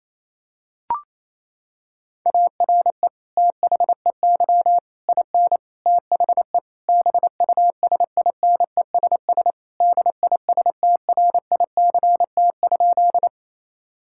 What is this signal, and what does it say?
Morse code, heard as a keyed tone, about 27 wpm, 700 Hz. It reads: ARE THEY IN THE BUSINESS DISTRICT?
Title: Sentences from Top 1600 Words 28wpm